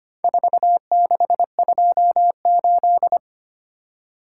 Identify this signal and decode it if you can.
Morse code, heard as a keyed tone, about 25 wpm, 700 Hz. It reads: 4628
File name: practice / 25wpm / Four Letter-Number Combinations - 25wpm